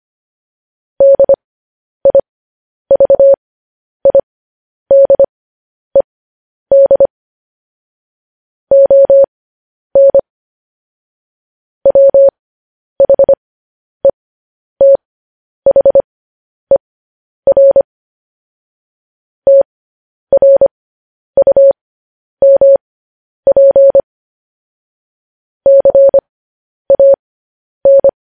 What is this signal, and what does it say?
Morse code, heard as a keyed tone, about 25 wpm, spaced as for 10 wpm, 550 Hz. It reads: DIVIDED ON WHETHER TRUMP CAN